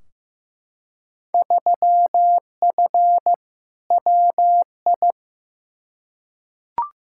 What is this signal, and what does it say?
Morse code, heard as a keyed tone, about 15 wpm, 700 Hz. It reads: 3F WI
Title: Field Day Exchanges - 15wpm